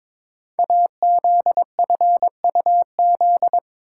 Morse code, heard as a keyed tone, about 22 wpm, 700 Hz. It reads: AZFUZ
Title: Code Groups - 5 Letters - 22wpm